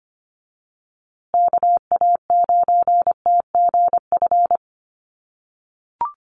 Morse code, heard as a keyed tone, about 25 wpm, 700 Hz. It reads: KA9TGF
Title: USA Call Signs Repeated 2x 25wpm